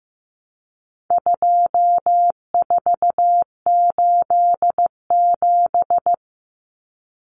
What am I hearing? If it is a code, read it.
Morse code, heard as a keyed tone, about 15 wpm, 700 Hz. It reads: 2487
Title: Numbers - 4-Digit 15wpm